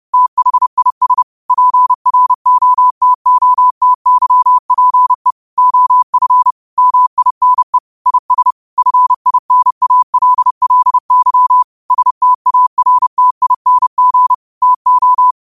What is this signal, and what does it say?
Morse code, heard as a keyed tone, about 30 wpm, 1000 Hz. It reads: THIS PROTOTYPE OF MINE IS FINALLY STARTING TO